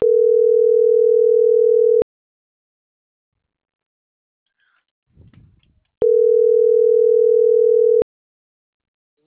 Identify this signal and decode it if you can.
Ringback tone (North America), 440 + 480 Hz.
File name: CENTRAL MAERSK / out gabrielle.melendez - .25657